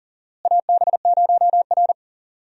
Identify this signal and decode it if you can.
Morse code, heard as a keyed tone, about 40 wpm, 700 Hz. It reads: AB0R
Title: USA Call Signs - 4 Characters - Repeated 2x - 40wpm